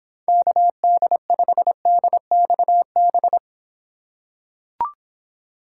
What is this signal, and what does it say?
Morse code, heard as a keyed tone, about 26 wpm, 700 Hz. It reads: KD5DXB